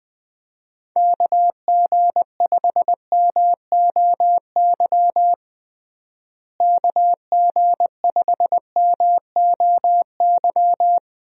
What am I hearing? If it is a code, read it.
Morse code, heard as a keyed tone, about 20 wpm, 700 Hz. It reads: KG5MOY KG5MOY
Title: USA Call Signs - 6 Characters - Repeated 2x - 20wpm